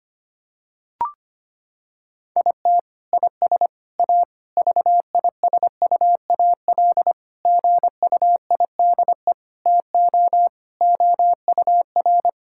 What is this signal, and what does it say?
Morse code, heard as a keyed tone, about 25 wpm, 700 Hz. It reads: IT IS A VISUAL GUIDE TO OUR
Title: Sentences from Top 1000 Words - Review Part 6 - 25wpm